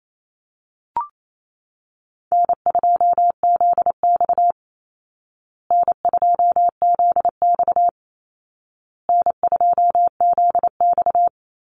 Morse code, heard as a keyed tone, about 28 wpm, 700 Hz. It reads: N2ZX N2ZX N2ZX
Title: Call Signs Repeated 2x - 28wpm